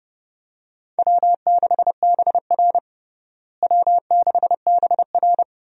Morse code, heard as a keyed tone, about 30 wpm, 700 Hz. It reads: W6BR W6BR
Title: USA Call Signs - 4 Characters - Repeated 2x - 30wpm